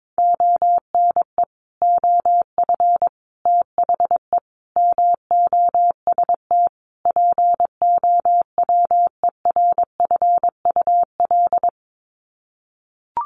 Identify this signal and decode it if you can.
Morse code, heard as a keyed tone, about 22 wpm, 700 Hz. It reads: ONE OF THE MOST POWERFUL
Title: Sets of 5 Words Encore 22wpm